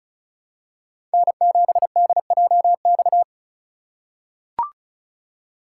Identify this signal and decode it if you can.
Morse code, heard as a keyed tone, about 34 wpm, 700 Hz. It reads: N7DJX
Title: USA Call Signs - 5 Characters - 3x Character Spacing - 35wpm